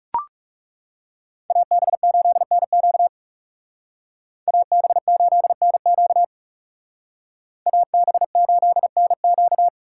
Morse code, heard as a keyed tone, about 37 wpm, 700 Hz. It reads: AB8NQ AB8NQ AB8NQ
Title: Speed Racing - USA 2x2 Call Signs 30wpm